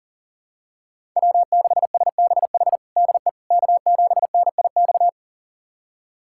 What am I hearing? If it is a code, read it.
Morse code, heard as a keyed tone, about 37 wpm, 700 Hz. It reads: W6SBH DE K7NIX
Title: USA Callsign DE Callsign 40wpm